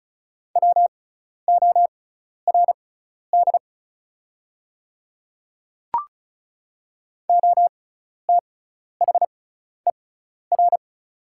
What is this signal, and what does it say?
Morse code, heard as a keyed tone, about 35 wpm, spaced as for 12 wpm, 690 Hz. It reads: WORD OTHER
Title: 6x Character Spacing - Top 200 Words - 35wpm